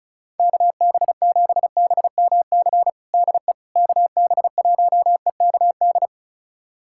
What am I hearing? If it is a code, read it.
Morse code, heard as a keyed tone, about 35 wpm, 700 Hz. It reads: KB7BMC DE KB1EKD
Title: USA Callsign DE Callsign - 6 Characters - 35wpm